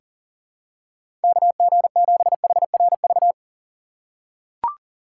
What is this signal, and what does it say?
Morse code, heard as a keyed tone, about 40 wpm, 700 Hz. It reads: KG7HRV
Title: USA Call Signs - 3x Character Spacing - 40wpm